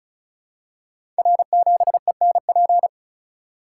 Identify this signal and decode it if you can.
Morse code, heard as a keyed tone, about 36 wpm, 700 Hz. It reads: R7ENP
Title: Code Groups - 5 Letter:Numbers - 35wpm